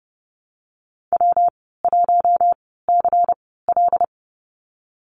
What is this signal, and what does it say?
Morse code, heard as a keyed tone, about 30 wpm, spaced as for 17 wpm, 700 Hz. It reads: W1CL
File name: course / 30:10wpm / 116 Lesson 19g - Letter w - Call Signs 30wpm